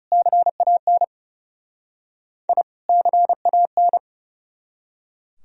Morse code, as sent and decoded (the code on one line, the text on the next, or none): -.-. .- -. / .. / -.-. .- -.
CAN I CAN